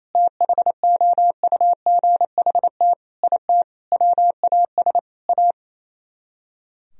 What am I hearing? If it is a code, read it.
Morse code, heard as a keyed tone, about 34 wpm, 700 Hz. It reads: THOUGHT IT WAS A
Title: Speed-Racing - Sets of 5 Words 28wpm